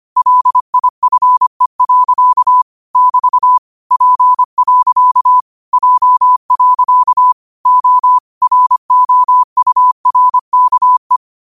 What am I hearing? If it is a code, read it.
Morse code, heard as a keyed tone, about 25 wpm, 1000 Hz. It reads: LIFE. <BT> P. J. O ROURKE
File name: 25wpm